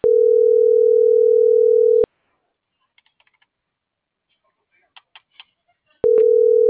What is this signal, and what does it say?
Ringback tone (North America), 440 + 480 Hz.